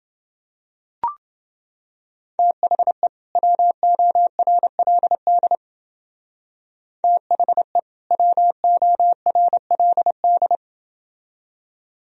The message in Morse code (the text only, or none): THE WORLD THE WORLD